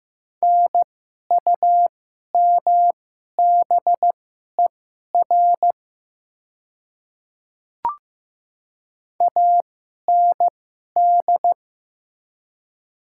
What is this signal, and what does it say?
Morse code, heard as a keyed tone, about 15 wpm, spaced as for 11 wpm, 690 Hz. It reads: NUMBER AND
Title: 2x Character Spacing - Top 100 Words - 15wpm